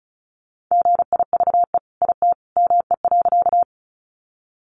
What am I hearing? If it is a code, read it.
Morse code, heard as a keyed tone, about 34 wpm, 700 Hz. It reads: GIVE IT ME.
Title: Sentences from Top 200 Words 35wpm